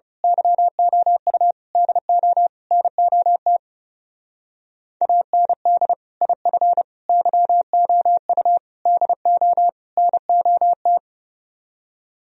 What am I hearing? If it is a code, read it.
Morse code, heard as a keyed tone, about 28 wpm, 700 Hz. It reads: YOU DO NOT AND IF YOU DO NOT
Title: Speed-Racing - Sets of 5 Words 30wpm